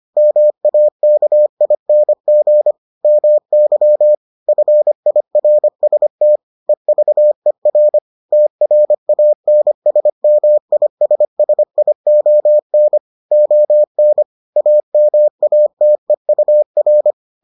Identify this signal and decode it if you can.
Morse code, heard as a keyed tone, about 25 wpm, 600 Hz. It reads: MAKING MY FIRST EVER TRANSMISSION ON AMATEUR